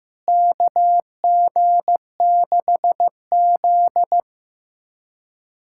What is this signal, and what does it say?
Morse code, heard as a keyed tone, about 15 wpm, 700 Hz. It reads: KG6Z
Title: USA Callsign DE Callsign - 4 Characters - 15wpm